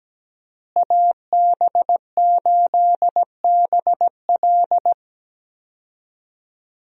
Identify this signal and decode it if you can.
Morse code, heard as a keyed tone, about 17 wpm, 700 Hz. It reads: AB8BL
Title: USA Call Signs - 5 Characters - Repeated 3x - 17wpm